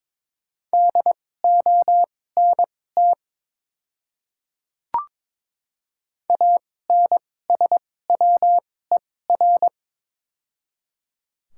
Morse code, heard as a keyed tone, about 23 wpm, spaced as for 16 wpm, 700 Hz. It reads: DONT ANSWER
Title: 2x Character Spacing - Top 300 Words - 22wpm